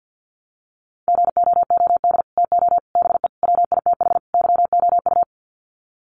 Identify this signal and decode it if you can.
Morse code, heard as a keyed tone, about 48 wpm, 700 Hz. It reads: GOOD TO BE WITH YOU
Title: Sets of 5 Words Encore2 50wpm